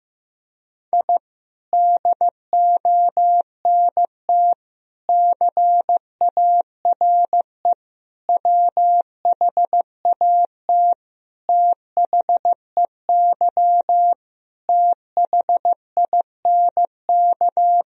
Morse code, heard as a keyed tone, about 15 wpm, 700 Hz. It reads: I DONT CARE WHAT THEY THINK
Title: Sentences from Top 1000 Words - Review Part 1 - 15wpm